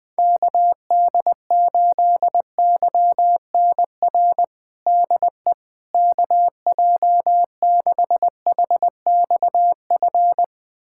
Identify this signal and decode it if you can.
Morse code, heard as a keyed tone, about 20 wpm, 700 Hz. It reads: KD8YNR DE KJ6HXF
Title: USA Callsign DE Callsign 20wpm